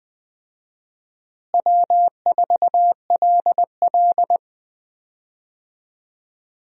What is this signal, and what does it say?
Morse code, heard as a keyed tone, about 21 wpm, 700 Hz. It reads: W4LL
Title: USA Call Signs - 4 Characters - Repeated 3x - 20wpm